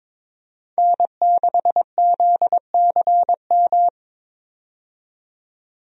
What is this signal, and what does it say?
Morse code, heard as a keyed tone, about 21 wpm, 700 Hz. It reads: N6ZCM
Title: USA Call Signs - 5 Characters - 22wpm